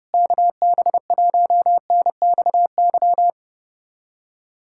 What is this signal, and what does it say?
Morse code, heard as a keyed tone, about 32 wpm, 700 Hz. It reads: KB1NXY
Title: USA Callsign DE Callsign - 6 Characters - 30wpm